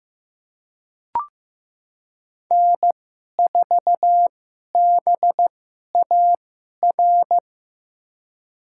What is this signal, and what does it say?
Morse code, heard as a keyed tone, about 15 wpm, spaced as for 11 wpm, 700 Hz. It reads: N4BAR
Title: USA Call Signs - 5 Characters - 2x Character Spacing - 15wpm